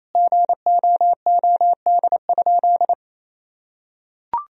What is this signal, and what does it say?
Morse code, heard as a keyed tone, about 27 wpm, 700 Hz. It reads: GOOD?
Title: Sentences from Top 1000 Words - Review Part 3 - 28wpm